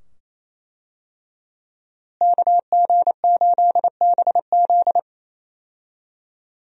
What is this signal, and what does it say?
Morse code, heard as a keyed tone, about 27 wpm, 700 Hz. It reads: KG8BZ
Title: USA Call Signs - 5 Characters - Repeated 2x - 28wpm